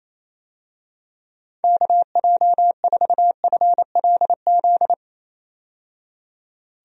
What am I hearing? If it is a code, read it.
Morse code, heard as a keyed tone, about 28 wpm, 700 Hz. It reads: KJ4FLZ
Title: USA Call Signs - 6 Characters - 28wpm